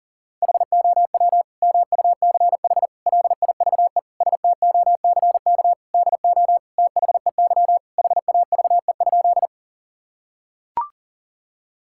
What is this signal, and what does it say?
Morse code, heard as a keyed tone, about 40 wpm, 700 Hz. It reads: HOW MUCH LIVE STOCK DO THEY HAVE?
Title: Sentences from Top 1000 Words - Review Part 5 - 40wpm